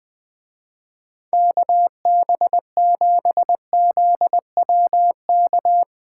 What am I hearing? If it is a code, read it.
Morse code, heard as a keyed tone, about 20 wpm, 700 Hz. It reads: KB7ZWK
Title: Speed Racing - USA 2x3 Call Signs 17wpm